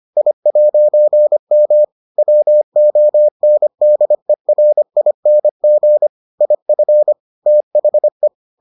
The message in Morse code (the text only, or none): I'M WONDERING IF THE